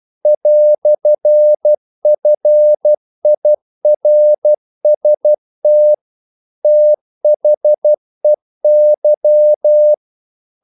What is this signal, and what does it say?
Morse code, heard as a keyed tone, about 12 wpm, 600 Hz. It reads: "FIRST THEY
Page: English quotations in CW (telegraphy - morse)